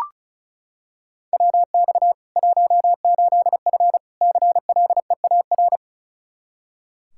Morse code, heard as a keyed tone, about 36 wpm, 700 Hz. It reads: WX 18F CLEAR
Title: Practice QSO - Exchange 2 - Info - 35wpm